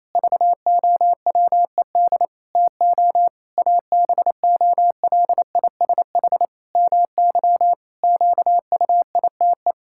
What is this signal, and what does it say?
Morse code, heard as a keyed tone, about 28 wpm, 700 Hz. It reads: VOWED TO ABOLISH MY QUITE
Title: Pangram Challenge v2 - 28wpm